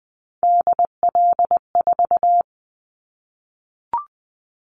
Morse code, heard as a keyed tone, about 20 wpm, 700 Hz. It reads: DL4
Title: Callsign Prefixes for Top 50 DXCC - 20wpm